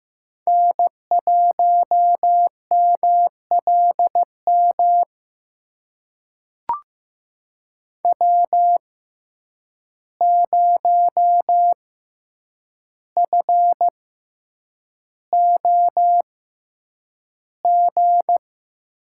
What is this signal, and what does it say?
Morse code, heard as a keyed tone, about 15 wpm, spaced as for 8 wpm, 700 Hz. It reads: N1MLM W0FOG
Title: USA Call Signs - 5 Characters - 6x Character Spacing - 15wpm